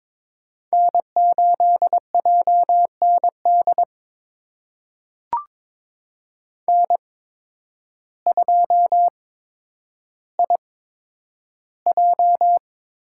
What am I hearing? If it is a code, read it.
Morse code, heard as a keyed tone, about 22 wpm, spaced as for 10 wpm, 700 Hz. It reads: N8JND N2IJ
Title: USA Call Signs - 5 Characters - 8x Character Spacing - 22wpm